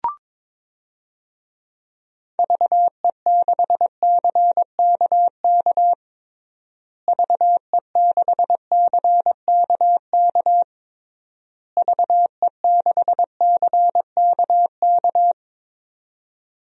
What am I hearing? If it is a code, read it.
Morse code, heard as a keyed tone, about 22 wpm, 700 Hz. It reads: VE6CKK VE6CKK VE6CKK
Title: DXCC-100 Call Signs - 6 Characters - Repeated 3x - 22wpm